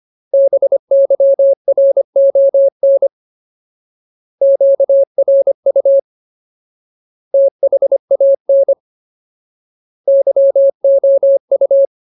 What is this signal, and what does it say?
Morse code, heard as a keyed tone, about 25 wpm, 550 Hz. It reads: BYRON QRU THAN YOU
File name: Ultimate Word List / 25WPM Ultimate